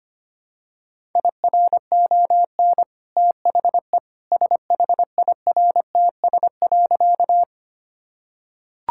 Morse code, heard as a keyed tone, about 25 wpm, 690 Hz. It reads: IRON THE SHIRTS.
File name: Easy Sentences with Hints - 25wpm